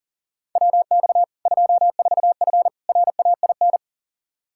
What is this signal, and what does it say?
Morse code, heard as a keyed tone, about 40 wpm, 700 Hz. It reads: WX 24F RAIN
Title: Practice QSO - Exchange 2 - Info - 40wpm